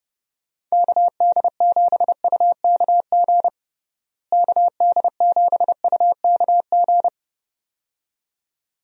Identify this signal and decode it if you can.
Morse code, heard as a keyed tone, about 32 wpm, 700 Hz. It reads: KD7UKG KD7UKG
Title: USA Call Signs - 6 Characters - Repeated 2x - 30wpm